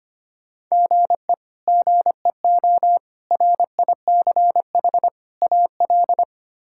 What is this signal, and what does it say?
Morse code, heard as a keyed tone, about 25 wpm, 700 Hz. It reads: GE GEO RICH AL
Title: K1USN SST QSO Practice - 25wpm